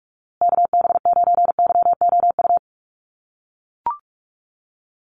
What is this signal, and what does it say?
Morse code, heard as a keyed tone, about 44 wpm, 700 Hz. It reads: KB9YOU
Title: USA Call Signs Repeated 2x 45wpm